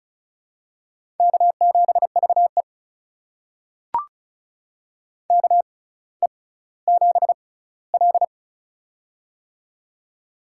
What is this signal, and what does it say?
Morse code, heard as a keyed tone, about 36 wpm, spaced as for 18 wpm, 700 Hz. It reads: K7VE KE7L
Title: USA Call Signs - 4 Characters - 6x Character Spacing - 35wpm